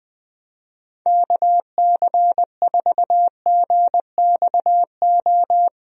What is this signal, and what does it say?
Morse code, heard as a keyed tone, about 20 wpm, 700 Hz. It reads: KC4GXO
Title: USA Call Signs - 6 Characters - Repeated 3x - 20wpm